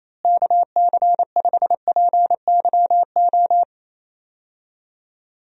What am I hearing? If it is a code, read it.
Morse code, heard as a keyed tone, about 27 wpm, 700 Hz. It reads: KC5PYO
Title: USA Call Signs - 6 Characters - Repeated 3x - 28wpm